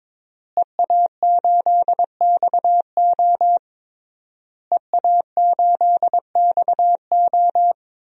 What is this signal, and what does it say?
Morse code, heard as a keyed tone, about 22 wpm, 700 Hz. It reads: EA8XO EA8XO
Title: DXCC-100 Call Signs - 5 Characters - Repeated 3x - 22wpm